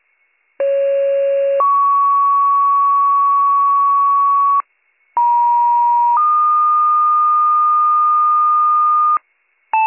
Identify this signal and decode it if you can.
Two-tone page, heard as a one-second tone, then a three-second tone, about 550 then 1120 Hz, about 950 then 1220 Hz. It is heard from 0.6 to 9.2 s.